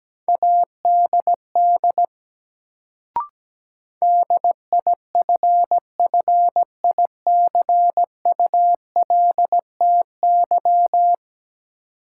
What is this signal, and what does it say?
Morse code, heard as a keyed tone, about 17 wpm, 700 Hz. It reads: ADD DIFFICULTY
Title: Sentences from Top 1500 Words - New Word Drill Alt - 17wpm